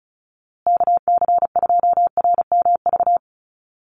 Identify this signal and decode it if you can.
Morse code, heard as a keyed tone, about 34 wpm, 700 Hz. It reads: KC2RMV
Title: USA Call Signs Repeated 3x 35wpm